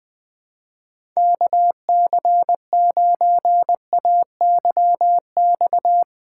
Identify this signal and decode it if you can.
Morse code, heard as a keyed tone, about 20 wpm, 700 Hz. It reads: KC9AYX
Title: USA Call Signs - 6 Characters - 2x Character Spacing - 20wpm